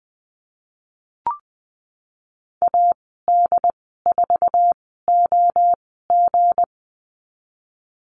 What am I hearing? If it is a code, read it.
Morse code, heard as a keyed tone, about 19 wpm, spaced as for 14 wpm, 700 Hz. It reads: AD4OG
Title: 173 Lesson 28g - Number 9 - Call Signs 20wpm